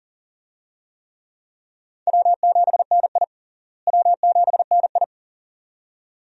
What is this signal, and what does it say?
Morse code, heard as a keyed tone, about 40 wpm, 700 Hz. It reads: W7NI W7NI
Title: USA Call Signs - 4 Characters - Repeated 2x - 40wpm